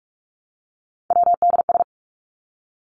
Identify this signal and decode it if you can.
Morse code, heard as a keyed tone, about 44 wpm, 700 Hz. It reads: WDS
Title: Abbreviations Repeated 3x - 45wpm